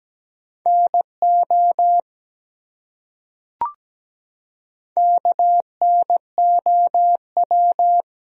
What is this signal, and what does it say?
Morse code, heard as a keyed tone, about 17 wpm, 700 Hz. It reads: NO KNOW
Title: Top 100 Words 17wpm